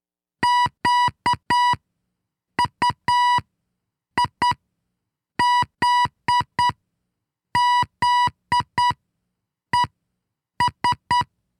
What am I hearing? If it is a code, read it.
Morse code, heard as a keyed tone, about 9 wpm, spaced as for 6 wpm, 980 Hz. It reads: QUIZZES
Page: This is the tenth transmission in our monthly Morse Code contest celebrating the centennial of Wisconsin Public Radio in 2017.